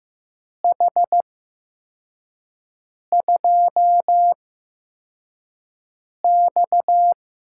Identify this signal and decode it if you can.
Morse code, heard as a keyed tone, about 15 wpm, spaced as for 4 wpm, 700 Hz. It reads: H2X
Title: USA Call Signs - 4 Characters - 8x Character Spacing - 15wpm